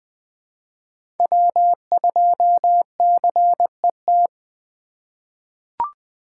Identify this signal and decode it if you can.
Morse code, heard as a keyed tone, about 20 wpm, 700 Hz. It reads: W2CET
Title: USA Call Signs - 5 Characters - 3x Character Spacing - 20wpm